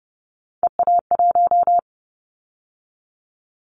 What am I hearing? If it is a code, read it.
Morse code, heard as a keyed tone, about 30 wpm, 700 Hz. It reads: EA1
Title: Callsign Prefixes for Top 25 DXCC - 30wpm